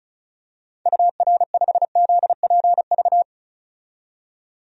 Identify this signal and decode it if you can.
Morse code, heard as a keyed tone, about 36 wpm, 700 Hz. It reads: UR5ZPV
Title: DXCC-100 Call Signs - 6 Characters - 35wpm